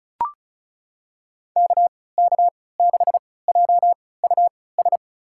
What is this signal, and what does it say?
Morse code, heard as a keyed tone, about 36 wpm, spaced as for 20 wpm, 700 Hz. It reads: KK6JUS
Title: USA Call Signs - 3x Character Spacing - 35wpm